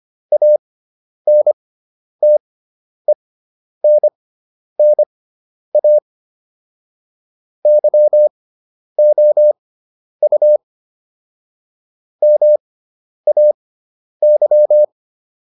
Morse code, heard as a keyed tone, about 25 wpm, spaced as for 10 wpm, 600 Hz. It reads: ANTENNA YOU MAY